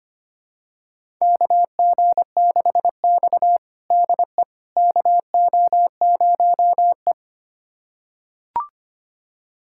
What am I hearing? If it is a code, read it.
Morse code, heard as a keyed tone, about 25 wpm, 700 Hz. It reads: KG6X DE KO0E
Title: USA Callsign DE Callsign - 4 Characters - 25wpm